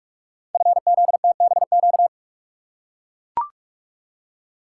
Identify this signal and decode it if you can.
Morse code, heard as a keyed tone, about 44 wpm, 700 Hz. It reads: FZTBQ
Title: Code Groups - 5 Letters - 45wpm